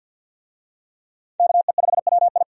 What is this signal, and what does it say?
Morse code, heard as a keyed tone, about 50 wpm, 700 Hz. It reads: KE5WI
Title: USA Call Signs - 5 Characters - 50wpm